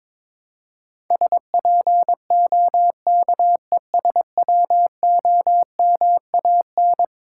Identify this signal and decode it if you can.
Morse code, heard as a keyed tone, about 22 wpm, 700 Hz. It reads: SPOKESWOMAN
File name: practice / 22wpm / Eleven Letter Words - 22wpm